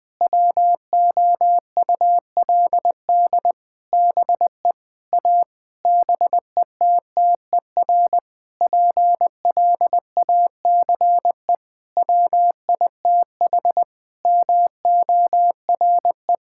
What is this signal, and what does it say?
Morse code, heard as a keyed tone, about 20 wpm, 700 Hz. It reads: WOULD BE A BETTER PLACE WITH MORE
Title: Sentences with 7-Word Phrases - Spoken Beginning - 20wpm